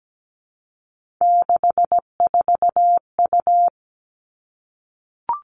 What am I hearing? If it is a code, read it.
Morse code, heard as a keyed tone, about 17 wpm, 700 Hz. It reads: 64U
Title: Three Challenging Letter-Number Combinations 17wpm